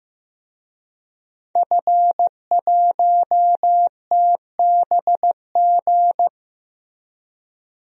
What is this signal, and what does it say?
Morse code, heard as a keyed tone, about 15 wpm, 700 Hz. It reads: F1TBG
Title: DXCC-100 Call Signs - 5 Characters - Repeated 3x - 15wpm